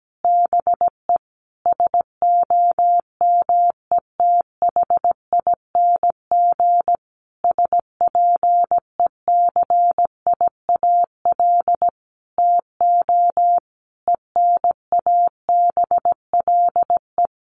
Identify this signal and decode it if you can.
Morse code, heard as a keyed tone, about 17 wpm, 700 Hz. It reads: BE SOMETHING SPECIAL TO ENABLE
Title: Sentences from Top 1500 Words 17wpm